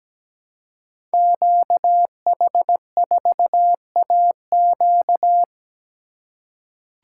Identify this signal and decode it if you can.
Morse code, heard as a keyed tone, about 17 wpm, 700 Hz. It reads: QH4AQ
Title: Code Groups - 5 Letter:Numbers - 17wpm